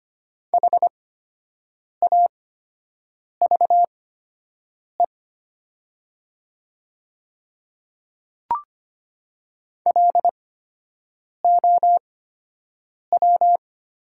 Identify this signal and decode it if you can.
Morse code, heard as a keyed tone, about 25 wpm, spaced as for 7 wpm, 700 Hz. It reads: HAVE LOW
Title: 8x Character Spacing - Top 300 Words - 25wpm